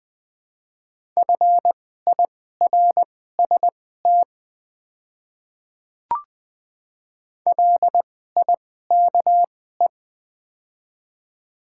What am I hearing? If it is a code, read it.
Morse code, heard as a keyed tone, about 20 wpm, spaced as for 15 wpm, 690 Hz. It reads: FIRST LIKE
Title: 2x Character Spacing - Top 100 Words - 20wpm